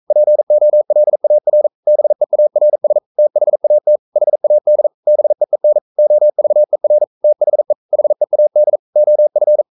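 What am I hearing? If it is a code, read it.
Morse code, heard as a keyed tone, about 42 wpm, 600 Hz. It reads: POLAR BEARS THAT HAD BEEN OVER THE HEAD OF
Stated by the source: Morse code sending of an English novel at 42 wpm.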